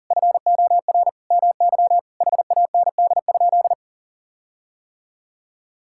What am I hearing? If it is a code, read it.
Morse code, heard as a keyed tone, about 37 wpm, 700 Hz. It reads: FOR MY HAND?
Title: Sentences from Top 1600 Words 40wpm